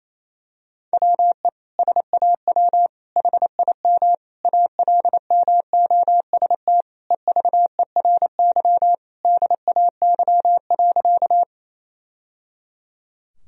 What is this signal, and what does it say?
Morse code, heard as a keyed tone, about 28 wpm, 700 Hz. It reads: WE SAW HIM ALMOST EVERY DAY.
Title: Sentences from Top 1000 Words - Review Part 4 - 28wpm